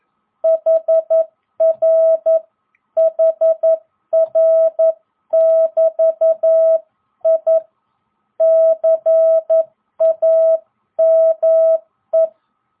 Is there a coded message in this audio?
Yes: Morse code from 0.4 to 12.3 s, heard as a keyed tone, about 11 wpm, 660 Hz. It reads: HR HR <BT> I CAME